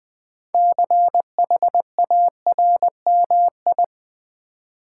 Morse code, heard as a keyed tone, about 20 wpm, 700 Hz. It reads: CHARMI
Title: Build A Word - 2 Letters At A Time - 8-Letter Words - 20wpm